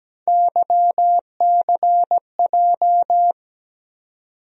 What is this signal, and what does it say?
Morse code, heard as a keyed tone, about 17 wpm, 700 Hz. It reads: YCJ
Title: USA Call Sign Postfixes - 3 Letters 17wpm